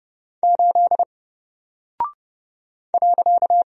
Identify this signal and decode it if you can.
Morse code, heard as a keyed tone, about 30 wpm, 700 Hz. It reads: 8 .
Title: Single Character (Letter:Number:Punctuation:Prosign) 30wpm